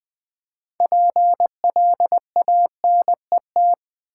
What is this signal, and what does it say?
Morse code, heard as a keyed tone, about 20 wpm, 700 Hz. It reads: PLANET
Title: Build A Word - 2 Letters At A Time - 6-Letter Words - 20wpm